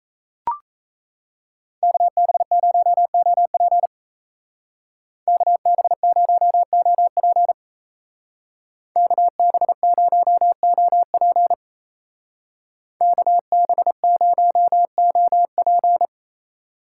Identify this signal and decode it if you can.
Morse code, heard as a keyed tone, about 33 wpm, 700 Hz. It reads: KB0OP KB0OP KB0OP KB0OP
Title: Speed Racing - USA Call Signs 28wpm